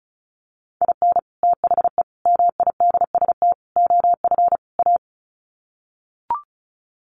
Morse code, heard as a keyed tone, about 34 wpm, 700 Hz. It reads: IN THE MIDST OF A
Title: Sets of 5 Words 35wpm